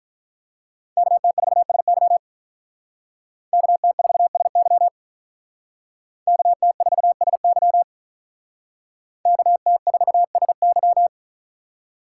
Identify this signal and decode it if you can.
Morse code, heard as a keyed tone, about 45 wpm, 700 Hz. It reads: KT4SY KT4SY KT4SY KT4SY